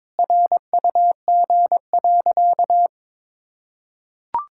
Morse code, heard as a keyed tone, about 22 wpm, 690 Hz. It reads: RUG.
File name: practice / 22wpm / Easy Sentences with Hints - 22wpm